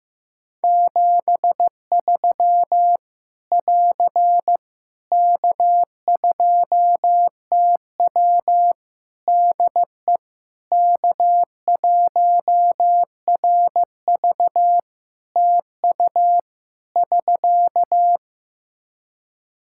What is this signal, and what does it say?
Morse code, heard as a keyed tone, about 15 wpm, 700 Hz. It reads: 73 <AR> K2TW DE K1RV TU <SK>
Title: K1USN SST QSO Practice 15wpm